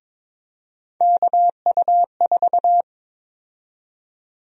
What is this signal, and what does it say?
Morse code, heard as a keyed tone, about 22 wpm, 700 Hz. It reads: KU4
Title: USA Call Sign Prefixes - 22wpm